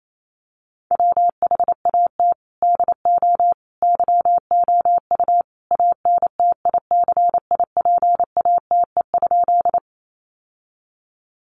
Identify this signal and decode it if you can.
Morse code, heard as a keyed tone, about 28 wpm, 700 Hz. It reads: WHAT DO YOU ANTICIPATE?
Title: Sentences from Top 1500 Words 28wpm